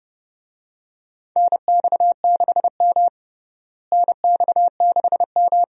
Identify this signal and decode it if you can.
Morse code, heard as a keyed tone, about 30 wpm, 700 Hz. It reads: NX6M NX6M
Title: USA Call Signs - 4 Characters - Repeated 2x - 30wpm